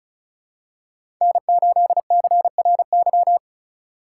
Morse code, heard as a keyed tone, about 36 wpm, 700 Hz. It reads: N8CRY
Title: USA Call Signs - 5 Characters - 4x Character Spacing - 35wpm